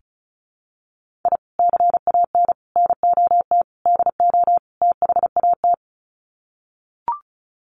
Morse code, heard as a keyed tone, about 34 wpm, 700 Hz. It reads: I CAN NOT DO THAT
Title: Sets of 5 Words Encore2 35wpm